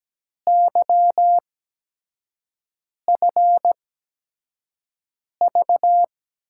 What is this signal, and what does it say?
Morse code, heard as a keyed tone, about 17 wpm, spaced as for 5 wpm, 700 Hz. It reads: YFV
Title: USA Call Signs - 8x Character Spacing - 17wpm